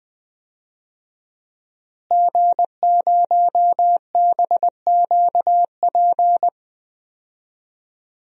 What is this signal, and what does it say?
Morse code, heard as a keyed tone, about 21 wpm, 700 Hz. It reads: G0BQP
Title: DXCC-100 Call Signs - 5 Characters - Repeated 3x - 20wpm